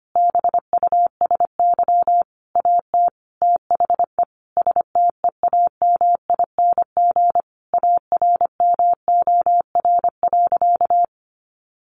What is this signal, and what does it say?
Morse code, heard as a keyed tone, about 25 wpm, 700 Hz. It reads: BUSY AT THE STEAMING ARMOR.
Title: A Sound Of Thunder 25wpm